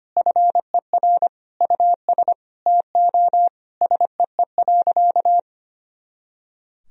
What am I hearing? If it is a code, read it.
Morse code, heard as a keyed tone, about 25 wpm, 700 Hz. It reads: FER US TO SEE.
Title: Sentences from Top 500 Words and Abbreviations 25wpm